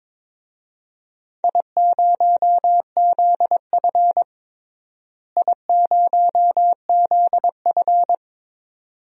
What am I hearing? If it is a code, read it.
Morse code, heard as a keyed tone, about 22 wpm, 700 Hz. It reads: I0ZF I0ZF
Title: DXCC-100 Call Signs - 4 Characters - Repeated 2x - 22wpm